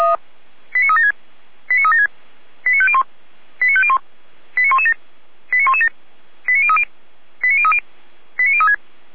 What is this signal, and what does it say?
Five-tone selective call (ZVEI), heard as quick runs of five tones, tones of about 70 ms. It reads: 89376 89376 89501 89501 89117 89117 89030 89030 89036